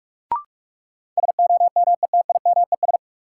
Morse code, heard as a keyed tone, about 44 wpm, 700 Hz. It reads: SOMETIMES
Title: Sentences from Top 600 Words - New Word Drill - 45wpm